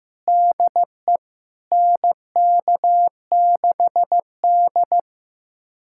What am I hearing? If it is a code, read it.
Morse code, heard as a keyed tone, about 15 wpm, 700 Hz. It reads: DE NK6D
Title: USA Callsign DE Callsign - 4 Characters - 15wpm